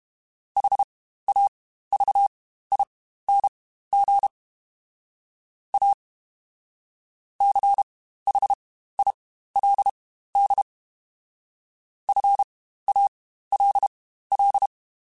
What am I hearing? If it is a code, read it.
Morse code, heard as a keyed tone, about 32 wpm, spaced as for 15 wpm, 800 Hz. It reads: HAVING A CHILD FALL
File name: Quotes Spaced / Quotes Spaced-mp3s / ChildAsleep sp @30